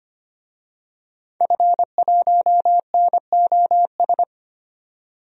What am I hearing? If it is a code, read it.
Morse code, heard as a keyed tone, about 25 wpm, 700 Hz. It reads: F1NOS
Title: DXCC-100 Call Signs - 5 Characters - 25wpm